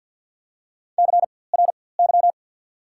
Morse code, heard as a keyed tone, about 54 wpm, spaced as for 24 wpm, 700 Hz. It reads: CRY
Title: 4x Character Spacing - Top 500 Words - 50wpm